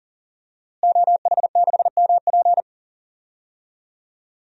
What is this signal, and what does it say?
Morse code, heard as a keyed tone, about 40 wpm, 700 Hz. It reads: OH6MP
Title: DXCC-100 Call Signs - 5 Characters - Repeated 3x - 40wpm